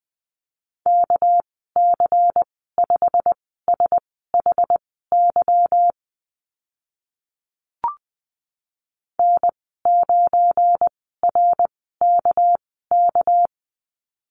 Morse code, heard as a keyed tone, about 20 wpm, spaced as for 14 wpm, 700 Hz. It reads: KC5SHY N9RKK
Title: 213 Lesson 34g - Number 8 - Call Signs 20wpm